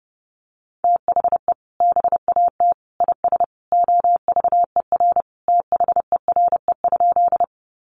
Morse code, heard as a keyed tone, about 30 wpm, 700 Hz. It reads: THE BAT IS OVER THERE?
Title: Sentences from Top 1500 Words 30wpm